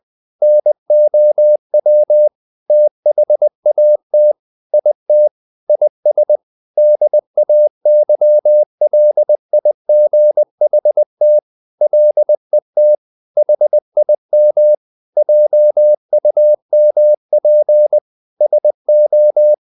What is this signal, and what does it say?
Morse code, heard as a keyed tone, about 20 wpm, 600 Hz. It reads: NOW THAT IT IS DAYLIGHT LET HIM JUMP SO
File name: the old man and the sea / the old man and the sea part 1 20wpm